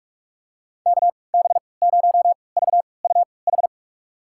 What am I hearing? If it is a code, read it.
Morse code, heard as a keyed tone, about 42 wpm, spaced as for 25 wpm, 700 Hz. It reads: KB0VUH
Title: USA Call Signs - 6 Characters - 3x Character Spacing - 45wpm